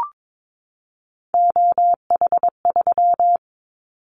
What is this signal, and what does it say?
Morse code, heard as a keyed tone, about 22 wpm, 700 Hz. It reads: OH3
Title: Callsign Prefixes for Top 50 DXCC - 22wpm